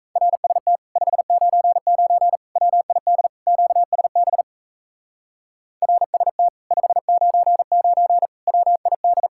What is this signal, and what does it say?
Morse code, heard as a keyed tone, about 35 wpm, 700 Hz. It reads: RST 599 WID QSB RST 599 WID